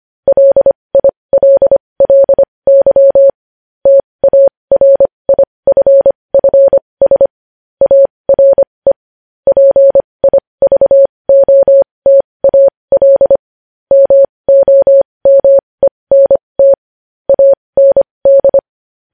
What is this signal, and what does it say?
Morse code, heard as a keyed tone, about 25 wpm, spaced as for 20 wpm, 550 Hz. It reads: LILLY TARIFFS ARE PIVOTAL MOMENT AND
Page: News headlines in morse code at 20 wpm